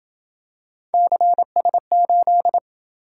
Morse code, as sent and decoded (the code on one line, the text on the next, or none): -.-. ... ---..
CS8